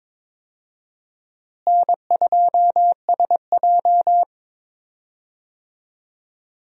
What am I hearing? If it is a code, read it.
Morse code, heard as a keyed tone, about 22 wpm, 700 Hz. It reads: N2SJ
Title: USA Call Signs - 4 Characters - Repeated 2x - 22wpm